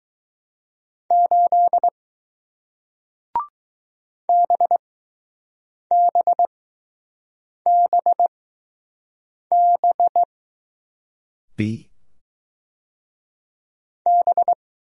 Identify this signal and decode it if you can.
Morse code, heard as a keyed tone, about 21 wpm, spaced as for 6 wpm, 700 Hz. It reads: 8 BBBB B